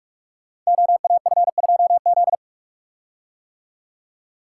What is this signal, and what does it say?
Morse code, heard as a keyed tone, about 44 wpm, 700 Hz. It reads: OAF2Z
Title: Code Groups - 5 Letter:Numbers - 45wpm